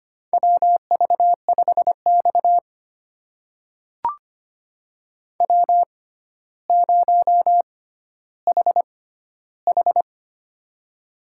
Morse code, heard as a keyed tone, about 25 wpm, spaced as for 13 wpm, 700 Hz. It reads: WV5X W0HH
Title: USA Call Signs - 6x Character Spacing - 25wpm